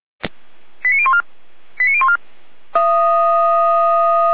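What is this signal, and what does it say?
Five-tone selective call (ZVEI), heard as quick runs of five tones, tones of about 70 ms. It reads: 89014 89014
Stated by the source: ZVEI five-tone call received by SDR, from the feuerwehr-orlamuende.org fire brigade server.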